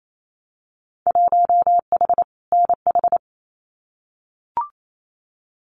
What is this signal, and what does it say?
Morse code, heard as a keyed tone, about 28 wpm, 700 Hz. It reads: 1H NH
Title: Winter Field Day Exchanges 28wpm